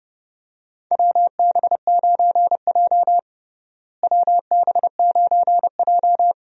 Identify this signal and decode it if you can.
Morse code, heard as a keyed tone, about 32 wpm, 700 Hz. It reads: WB9J WB9J
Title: USA Call Signs - 4 Characters - Repeated 3x - 30wpm